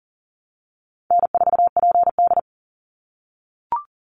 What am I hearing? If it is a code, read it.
Morse code, heard as a keyed tone, about 40 wpm, 700 Hz. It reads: N4PD
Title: Easy Call Signs - 40wpm